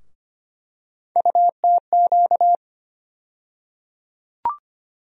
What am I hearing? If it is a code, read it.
Morse code, heard as a keyed tone, about 25 wpm, 700 Hz. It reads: UTQ
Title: USA Call Sign Postfixes - 3 Letters 25wpm